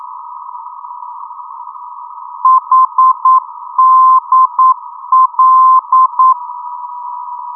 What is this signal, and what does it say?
Morse code, heard as a keyed tone, about 9 wpm, 1040 Hz. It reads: HDL
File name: europe / d-8060-cw-id-2004-rb